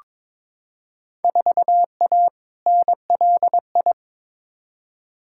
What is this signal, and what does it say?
Morse code, heard as a keyed tone, about 22 wpm, 700 Hz. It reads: 4A NLI
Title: Field Day Exchanges Encore - 22wpm